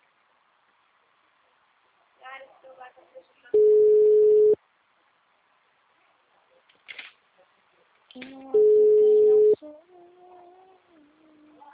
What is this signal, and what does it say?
Ringback tone (Europe, 425 Hz), with speech.
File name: out- -2013- - - .54